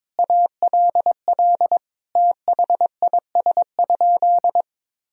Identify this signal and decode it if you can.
Morse code, heard as a keyed tone, about 22 wpm, 700 Hz. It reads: ALL THIS?
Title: Sentences from Top 1000 Words - Review Part 2 - 22wpm